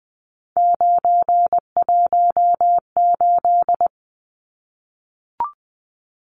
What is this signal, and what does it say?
Morse code, heard as a keyed tone, about 20 wpm, 700 Hz. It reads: 918
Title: Numbers - 3-Digit 20wpm